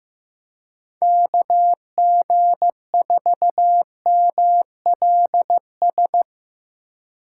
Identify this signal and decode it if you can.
Morse code, heard as a keyed tone, about 17 wpm, 700 Hz. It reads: KG4MLS
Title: Speed Racing - USA Call Signs 15wpm